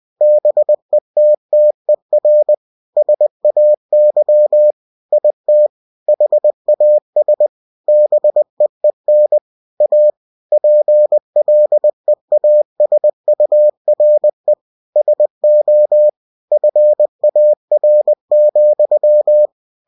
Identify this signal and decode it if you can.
Morse code, heard as a keyed tone, about 20 wpm, 600 Hz. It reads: BETTER SAY IT HAS BEEN A PLEASURE SO FAR,
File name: cw-books / winnie-the-pooh / winnie-20wpm-page-0107